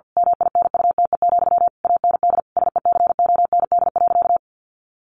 Morse code, heard as a keyed tone, about 50 wpm, 700 Hz. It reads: MINUTE, AND SECOND.
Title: A Sound Of Thunder 50wpm